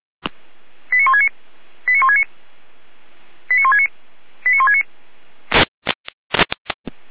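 Five-tone selective call (ZVEI), heard as quick runs of five tones, tones of about 70 ms. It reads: 89169 89169 89169 89169
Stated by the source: ZVEI five-tone call received on SDR.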